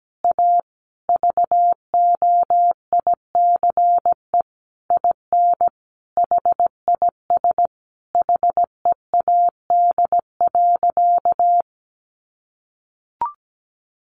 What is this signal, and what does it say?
Morse code, heard as a keyed tone, about 17 wpm, 700 Hz. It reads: A VOICE IN HIS HEAD.
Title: Sentences from Top 1300 Words 17wpm